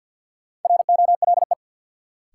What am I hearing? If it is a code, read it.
Morse code, heard as a keyed tone, about 53 wpm, 700 Hz. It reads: ROLE
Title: Sentences from Top 1000 Words - Review Part 6 - New Word Drill - 50wpm